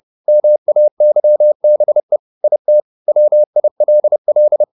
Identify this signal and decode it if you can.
Morse code, heard as a keyed tone, about 30 wpm, 600 Hz. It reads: MAYBE IT WILL
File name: the old man and the sea / the old man and the sea 1 30wpm